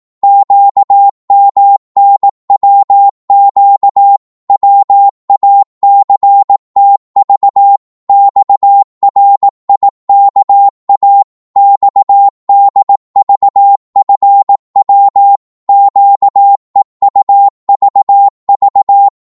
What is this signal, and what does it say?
Morse code, heard as a keyed tone, about 18 wpm, 800 Hz. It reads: QMNWQ WACTV XRIKA XDVFW QEUVV
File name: Morse / ltr-18wpm